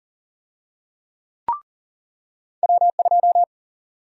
Morse code, heard as a keyed tone, about 40 wpm, 700 Hz. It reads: W2
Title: USA Call Sign Prefixes - 40wpm